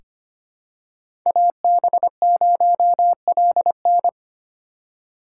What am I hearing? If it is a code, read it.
Morse code, heard as a keyed tone, about 25 wpm, 700 Hz. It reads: AB0LN
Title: USA Call Signs - 5 Characters - 4x Character Spacing - 25wpm